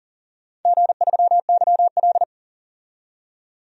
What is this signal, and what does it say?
Morse code, heard as a keyed tone, about 37 wpm, 700 Hz. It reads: G3YL
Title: DXCC-100 Call Signs - 4 Characters - 40wpm